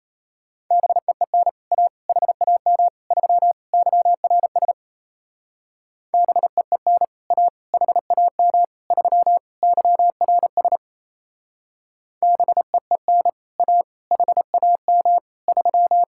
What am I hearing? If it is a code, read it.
Morse code, heard as a keyed tone, about 33 wpm, 700 Hz. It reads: BEEN A HAM 3 YRS BEEN A HAM 3 YRS BEEN A HAM 3
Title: Speed-Racing Common QSO Phrases Encore - 28wpm